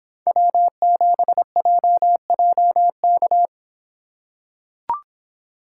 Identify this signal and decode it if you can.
Morse code, heard as a keyed tone, about 24 wpm, 700 Hz. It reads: W7JJK